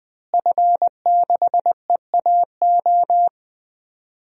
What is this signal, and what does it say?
Morse code, heard as a keyed tone, about 20 wpm, 700 Hz. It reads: F6EAO
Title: DXCC-100 Call Signs - 5 Characters - Repeated 2x - 20wpm